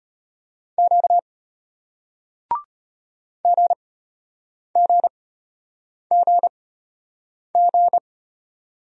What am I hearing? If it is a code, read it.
Morse code, heard as a keyed tone, about 31 wpm, spaced as for 8 wpm, 700 Hz. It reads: Q GGGG
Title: Speed-Racing Single Letters 25wpm